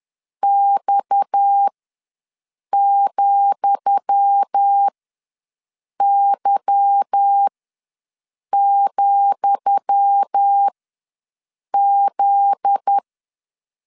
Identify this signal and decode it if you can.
Morse code, heard as a keyed tone, about 10 wpm, spaced as for 6 wpm, 800 Hz. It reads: X,Y,Z
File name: LEZIONI / 10 SIGNS / 10D COMA ws 10wpm 4314